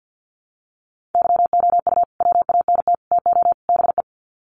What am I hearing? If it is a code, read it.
Morse code, heard as a keyed tone, about 50 wpm, 700 Hz. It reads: YOU WANT TO BE
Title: Sets of 4 Words 50wpm